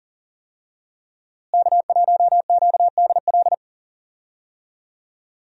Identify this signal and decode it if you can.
Morse code, heard as a keyed tone, about 40 wpm, 700 Hz. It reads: K1QDL
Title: USA Call Signs - 5 Characters - Repeated 3x - 40wpm